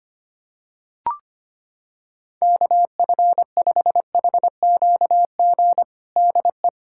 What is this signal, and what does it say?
Morse code, heard as a keyed tone, about 25 wpm, 700 Hz. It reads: KF5HQG DE
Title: USA Callsign DE Callsign 25wpm